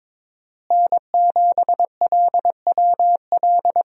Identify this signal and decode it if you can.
Morse code, heard as a keyed tone, about 22 wpm, 700 Hz. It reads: N7LWL
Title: USA Call Signs - 5 Characters - Repeated 3x - 22wpm